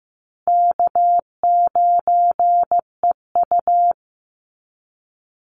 Morse code, heard as a keyed tone, about 15 wpm, 700 Hz. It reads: K9EU
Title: Easy Call Signs Repeated 2x - 15wpm